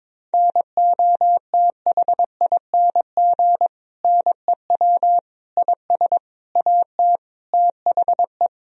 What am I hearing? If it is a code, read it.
Morse code, heard as a keyed tone, about 22 wpm, 700 Hz. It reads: NOTHING NEW IS AT THE
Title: Sentences from Top 1000 Words - Review Part 1 - 22wpm